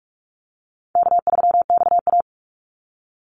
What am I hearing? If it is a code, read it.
Morse code, heard as a keyed tone, about 44 wpm, 700 Hz. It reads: K3XA
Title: Easy Call Signs - 45wpm